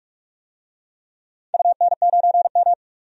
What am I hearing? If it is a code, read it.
Morse code, heard as a keyed tone, about 47 wpm, 700 Hz. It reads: UN9M
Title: DXCC-100 Call Signs - 4 Characters - 45wpm